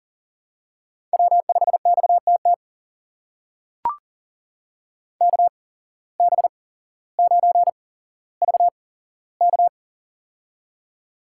Morse code, heard as a keyed tone, about 42 wpm, spaced as for 17 wpm, 700 Hz. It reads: W5XTT KB9VK
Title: USA Call Signs - 5 Characters - 8x Character Spacing - 40wpm